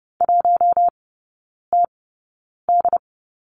Morse code, heard as a keyed tone, about 30 wpm, spaced as for 9 wpm, 700 Hz. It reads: 1TD
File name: Lesson 15e - Number 5 - Three Characters 30wpm